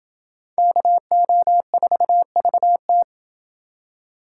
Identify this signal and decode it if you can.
Morse code, heard as a keyed tone, about 29 wpm, 700 Hz. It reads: KO4VT